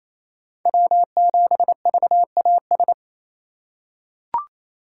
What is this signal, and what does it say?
Morse code, heard as a keyed tone, about 28 wpm, 700 Hz. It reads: W7VAS
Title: K1USN SST QSO Practice - 28wpm